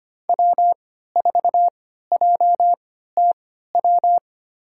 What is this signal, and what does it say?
Morse code, heard as a keyed tone, about 25 wpm, spaced as for 14 wpm, 700 Hz. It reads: W4JTW
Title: USA Call Signs - 5 Characters - 3x Character Spacing - 25wpm